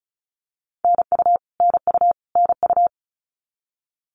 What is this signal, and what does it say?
Morse code, heard as a keyed tone, about 34 wpm, 700 Hz. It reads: NU NU NU
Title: Canadian Provinces and Territories - Repeated 3x - 35wpm